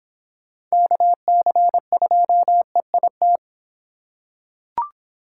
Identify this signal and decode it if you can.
Morse code, heard as a keyed tone, about 24 wpm, 700 Hz. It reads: KC2EIT